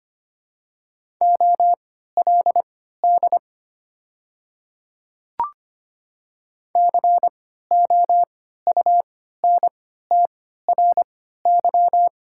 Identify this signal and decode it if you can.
Morse code, heard as a keyed tone, about 25 wpm, spaced as for 14 wpm, 690 Hz. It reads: OLD COUNTRY
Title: 3x Character Spacing - Top 300 Words - 25wpm